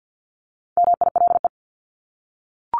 Morse code, heard as a keyed tone, about 50 wpm, 700 Hz. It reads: MILE
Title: ES:S Suffix - Top 100 Words with Base Words 50wpm